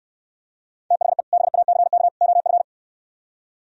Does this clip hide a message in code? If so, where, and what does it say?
Yes: Morse code from 0.9 to 2.6 s, heard as a keyed tone, about 68 wpm, 700 Hz. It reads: THE BACK OF